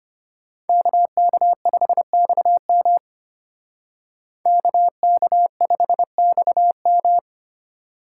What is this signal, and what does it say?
Morse code, heard as a keyed tone, about 32 wpm, 700 Hz. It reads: KK5XM KK5XM